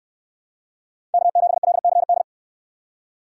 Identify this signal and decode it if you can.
Morse code, heard as a keyed tone, about 69 wpm, 700 Hz. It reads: K7LCD